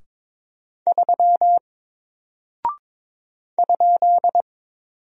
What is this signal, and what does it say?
Morse code, heard as a keyed tone, about 22 wpm, 700 Hz. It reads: 3 ?
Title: Single Character (Letter:Number:Punctuation:Prosign) 22wpm